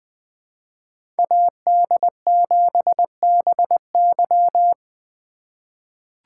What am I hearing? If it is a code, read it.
Morse code, heard as a keyed tone, about 20 wpm, 700 Hz. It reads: AD7BY
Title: USA Call Signs - 5 Characters - 2x Character Spacing - 20wpm